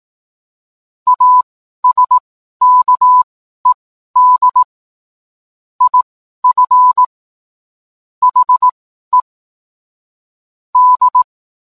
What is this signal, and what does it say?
Morse code, heard as a keyed tone, about 18 wpm, spaced as for 12 wpm, 1000 Hz. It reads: ASKED IF HE D